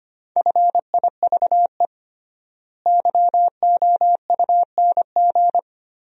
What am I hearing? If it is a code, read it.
Morse code, heard as a keyed tone, about 25 wpm, 700 Hz. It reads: FIVE YOUNG
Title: 3x Word-Spacing - Sentences from Top 400 Words - 25wpm